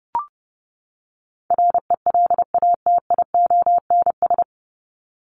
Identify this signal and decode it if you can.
Morse code, heard as a keyed tone, about 30 wpm, 700 Hz. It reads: RELATIONS
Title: ES:S Suffix - Top 100 Words with Base Words 30wpm